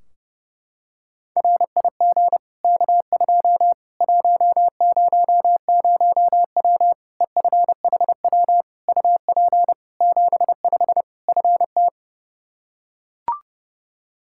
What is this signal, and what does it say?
Morse code, heard as a keyed tone, about 32 wpm, 700 Hz. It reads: RIG K2 100W EFHW UP 75 FT
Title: Practice QSO - Exchange 2 - Info - 30wpm